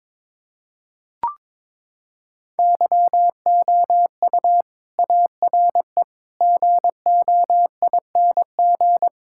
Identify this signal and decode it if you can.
Morse code, heard as a keyed tone, about 22 wpm, 700 Hz. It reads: YOU ARE GOING
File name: practice / 22wpm / Sentences with 4-Word Phrases - 22wpm